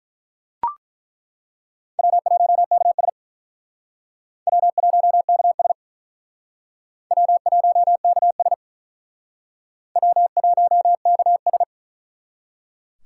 Morse code, heard as a keyed tone, about 42 wpm, 700 Hz. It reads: W1KS W1KS W1KS W1KS
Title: Speed Racing - USA 1x2 Call Signs 35wpm